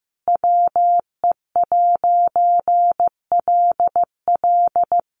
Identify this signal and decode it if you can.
Morse code, heard as a keyed tone, about 15 wpm, 700 Hz. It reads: WE'LL
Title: A Sound Of Thunder 15wpm